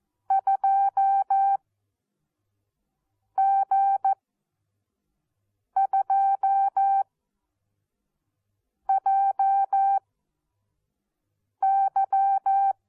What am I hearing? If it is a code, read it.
Morse code, heard as a keyed tone, about 14 wpm, spaced as for 4 wpm, 780 Hz. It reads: 2G2JY
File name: Onlinelehrgang / morsen / DJ4UF Morsen14